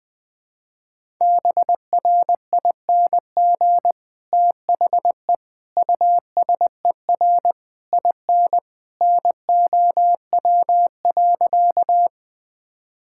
Morse code, heard as a keyed tone, about 20 wpm, 700 Hz. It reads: BRING THE USER IN NOW.
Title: Sentences from Top 1000 Words - Review Part 5 - 20wpm